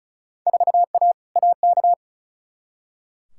Morse code, heard as a keyed tone, about 34 wpm, 700 Hz. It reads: 4A AK
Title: Field Day Exchanges - 35wpm